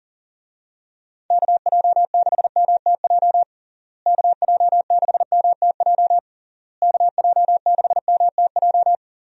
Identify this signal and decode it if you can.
Morse code, heard as a keyed tone, about 40 wpm, 700 Hz. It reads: KJ6MTJ KJ6MTJ KJ6MTJ
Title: USA Call Signs - 6 Characters - Repeated 3x - 40wpm